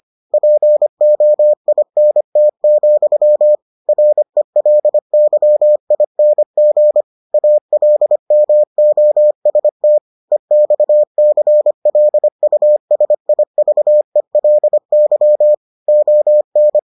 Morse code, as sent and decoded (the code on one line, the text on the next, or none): .--. --- .. -. - --..-- / .-. .-.. -.-- .. -. --. / .- .-.. -- --- ... - / . -..- -.-. .-.. ..- ... .. ...- . .-.. -.-- / --- -.
POINT, RELYING ALMOST EXCLUSIVELY ON